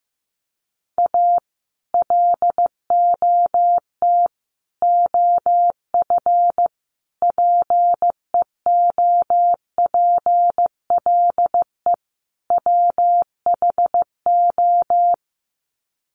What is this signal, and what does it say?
Morse code, heard as a keyed tone, about 15 wpm, 700 Hz. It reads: A LOT OF PEOPLE WHO
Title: Sets of 5 Words 15wpm